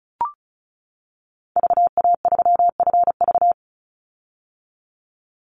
Morse code, heard as a keyed tone, about 34 wpm, 700 Hz. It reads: VA3FV
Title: Call Signs - 35wpm